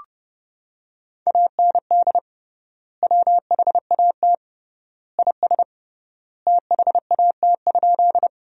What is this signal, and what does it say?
Morse code, heard as a keyed tone, about 32 wpm, 700 Hz. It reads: AND WHAT IS THAT?
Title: 3x Word-Spacing - Sentences from Top 100 Words - 30wpm